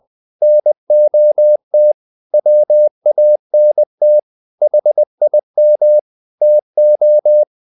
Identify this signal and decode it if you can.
Morse code, heard as a keyed tone, about 20 wpm, 600 Hz. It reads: NOT WANT HIM TO
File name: the old man and the sea part 2 20wpm